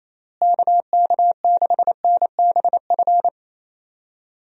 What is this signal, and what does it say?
Morse code, heard as a keyed tone, about 28 wpm, 700 Hz. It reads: KK6NBF
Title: USA Call Signs - 6 Characters - Repeated 2x - 28wpm